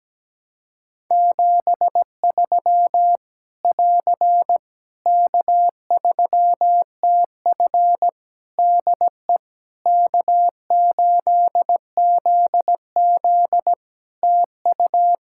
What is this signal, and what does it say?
Morse code, heard as a keyed tone, about 17 wpm, 700 Hz. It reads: 73 <AR> K3TF DE K8ZZ TU
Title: K1USN SST QSO Practice 17wpm